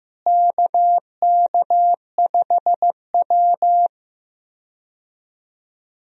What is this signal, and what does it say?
Morse code, heard as a keyed tone, about 15 wpm, 700 Hz. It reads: KK5W
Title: USA Call Signs - 4 Characters - Repeated 2x - 15wpm